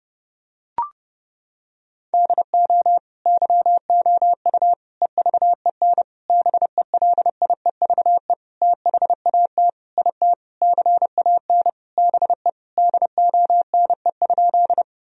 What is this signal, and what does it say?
Morse code, heard as a keyed tone, about 32 wpm, 700 Hz. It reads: DO YOU EVEN BELIEVE THAT IT CAN BE DONE?
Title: Sentences from Top 1000 Words - Review Part 1 - 30wpm